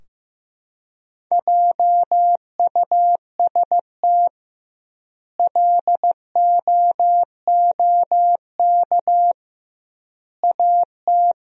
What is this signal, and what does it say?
Morse code, heard as a keyed tone, about 16 wpm, 700 Hz. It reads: JUST LOOK AT
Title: 2x Word-Spacing - Sentences from Top 200 Words - 15wpm